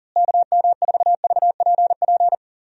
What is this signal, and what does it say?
Morse code, heard as a keyed tone, about 40 wpm, 700 Hz. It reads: KM4VPP
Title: USA Call Signs - 6 Characters - Repeated 3x - 40wpm